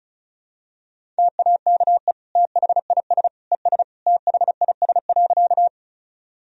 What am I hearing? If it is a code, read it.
Morse code, heard as a keyed tone, about 36 wpm, 700 Hz. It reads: TAKE THIS ES THIS.
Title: Sentences from Top 200 Words and Abbreviations 35wpm